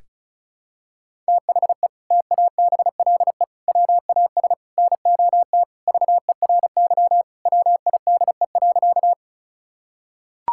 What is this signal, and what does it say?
Morse code, heard as a keyed tone, about 36 wpm, 700 Hz. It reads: THE TABLE WAS NOT VERY WIDE.
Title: Sentences from Top 1000 Words - Review Part 6 - 35wpm